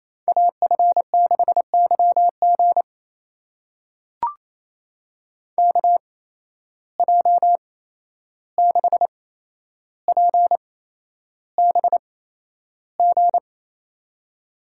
Morse code, heard as a keyed tone, about 28 wpm, spaced as for 11 wpm, 700 Hz. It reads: AF6YG KJ6PBG
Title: USA Call Signs - 8x Character Spacing - 28wpm